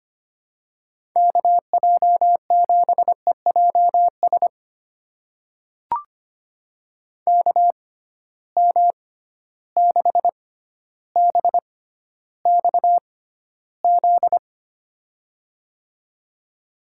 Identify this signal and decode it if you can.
Morse code, heard as a keyed tone, about 25 wpm, spaced as for 13 wpm, 700 Hz. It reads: KJ7EJS KM6BXZ
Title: USA Call Signs - 6 Characters - 6x Character Spacing - 25wpm